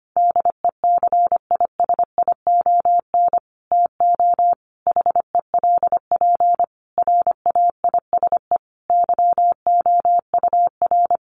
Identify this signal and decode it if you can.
Morse code, heard as a keyed tone, about 25 wpm, 700 Hz. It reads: DECISION TO HELP RAISE YOUR
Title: Sentences from Top 1400 Words 25wpm